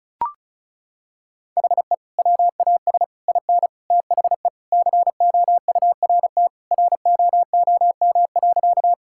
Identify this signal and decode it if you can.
Morse code, heard as a keyed tone, about 34 wpm, 700 Hz. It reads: HE WAS IN THE COURT ROOM.
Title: Sentences from Top 1600 Words 35wpm